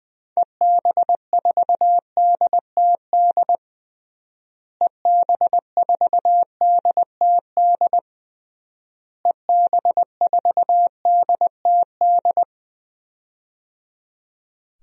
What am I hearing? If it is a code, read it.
Morse code, heard as a keyed tone, about 20 wpm, 700 Hz. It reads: EB4DTD EB4DTD EB4DTD
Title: DXCC-100 Call Signs - 6 Characters - Repeated 3x - 20wpm